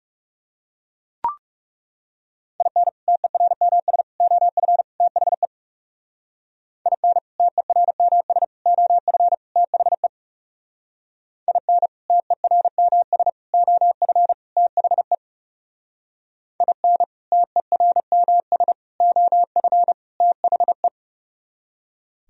Morse code, heard as a keyed tone, about 37 wpm, 700 Hz. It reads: IN TERMS OF THE IN TERMS OF THE IN TERMS OF THE IN TERMS OF THE
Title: Speed-Racing - Sets of 4 Words 30wpm